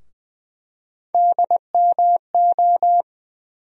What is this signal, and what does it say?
Morse code, heard as a keyed tone, about 20 wpm, 700 Hz. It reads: DMO
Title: USA Call Sign Postfixes - 3 Letters 20wpm